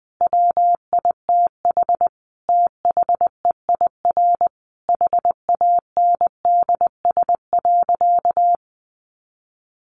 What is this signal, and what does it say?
Morse code, heard as a keyed tone, about 20 wpm, 700 Hz. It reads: WITH THEIR HANDS.
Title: Easy Phrases 20wpm